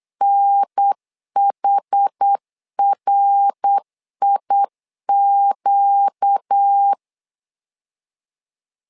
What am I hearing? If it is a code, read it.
Morse code, heard as a keyed tone, about 8 wpm, 800 Hz. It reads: NHRIQ
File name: 4 Q 8wpm